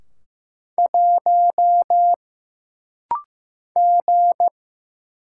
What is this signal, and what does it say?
Morse code, heard as a keyed tone, about 15 wpm, 700 Hz. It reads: 1 G
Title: Single Letter-Number Alt - Rapid-Fire 15wpm